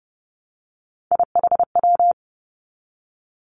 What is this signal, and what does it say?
Morse code, heard as a keyed tone, about 31 wpm, 700 Hz. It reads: IHW
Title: Three Challenging Letter-Number Combinations 30wpm